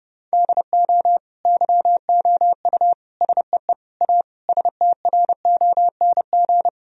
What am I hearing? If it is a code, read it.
Morse code, heard as a keyed tone, about 30 wpm, 700 Hz. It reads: DO YOU SEE A STRONG
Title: Sentences from Top 1000 Words - Review Part 4 - 30wpm